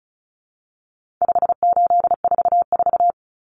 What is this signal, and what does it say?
Morse code, heard as a keyed tone, about 34 wpm, 700 Hz. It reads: 5844
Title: Common Cut Numbers - 4-Digit 35wpm